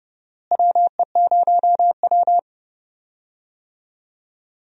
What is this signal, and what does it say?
Morse code, heard as a keyed tone, about 29 wpm, 700 Hz. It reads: WE0W
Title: USA Call Signs - 4 Characters - Repeated 3x - 30wpm